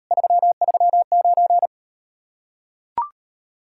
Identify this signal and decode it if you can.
Morse code, heard as a keyed tone, about 38 wpm, 700 Hz. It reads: 339